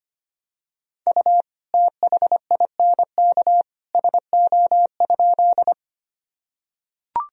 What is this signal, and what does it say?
Morse code, heard as a keyed tone, about 25 wpm, 700 Hz. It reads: U THINK SO?
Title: Sentences from Top 200 Words and Abbreviations 25wpm